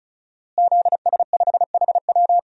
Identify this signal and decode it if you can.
Morse code, heard as a keyed tone, about 36 wpm, 700 Hz. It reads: ZS5HW
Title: DXCC-100 Call Signs - 5 Characters - Repeated 3x - 35wpm